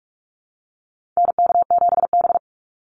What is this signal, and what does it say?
Morse code, heard as a keyed tone, about 44 wpm, 700 Hz. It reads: NK7B
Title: Call Signs Repeated 2x - 45wpm